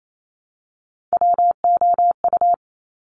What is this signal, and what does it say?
Morse code, heard as a keyed tone, about 28 wpm, 690 Hz. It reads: WOU
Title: Top Three Letter Combinations in Words 28wpm